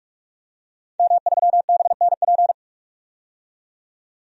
Morse code, heard as a keyed tone, about 46 wpm, 700 Hz. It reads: M3BNP
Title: DXCC-100 Call Signs - 5 Characters - 45wpm